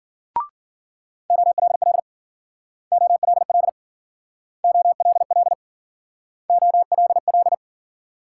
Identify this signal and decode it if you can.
Morse code, heard as a keyed tone, about 53 wpm, 700 Hz. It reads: OLL OLL OLL OLL